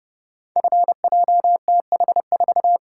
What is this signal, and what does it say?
Morse code, heard as a keyed tone, about 30 wpm, 700 Hz. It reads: FJTH4
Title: Code Groups - 5 Letter:Numbers - 30wpm